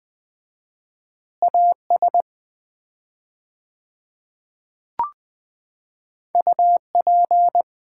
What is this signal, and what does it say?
Morse code, heard as a keyed tone, about 20 wpm, 700 Hz. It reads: AS UP
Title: 4x Word-Spacing - Sets of 3 Words - 20wpm